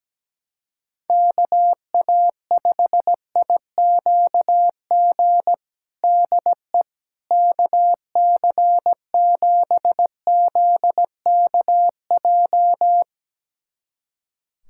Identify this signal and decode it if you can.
Morse code, heard as a keyed tone, about 17 wpm, 700 Hz. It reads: KA5IQG DE KC7ZKJ
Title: USA Callsign DE Callsign - 6 Characters - 17wpm